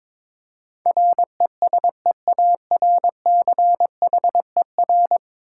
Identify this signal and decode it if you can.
Morse code, heard as a keyed tone, about 22 wpm, 700 Hz. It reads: RESEARCHER
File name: practice / 22wpm / 10 Letter Words - First 3 Letters Given - 22wpm